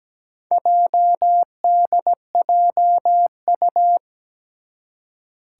Morse code, heard as a keyed tone, about 17 wpm, 700 Hz. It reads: JDJU
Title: Four Challenging Letter-Number Combinations 17wpm